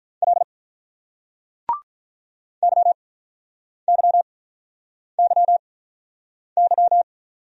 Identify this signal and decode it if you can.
Morse code, heard as a keyed tone, about 47 wpm, 700 Hz. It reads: L Y Y Y Y